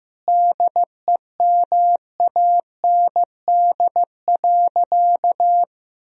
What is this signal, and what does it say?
Morse code, heard as a keyed tone, about 15 wpm, 700 Hz. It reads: DEMAND.
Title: Sentences from Top 1000 Words - Review Part 4 - 15wpm